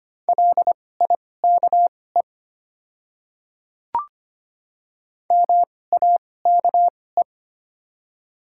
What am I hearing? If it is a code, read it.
Morse code, heard as a keyed tone, about 25 wpm, spaced as for 18 wpm, 700 Hz. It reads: LIKE MAKE
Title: 2x Character Spacing - Top 100 Words - 25wpm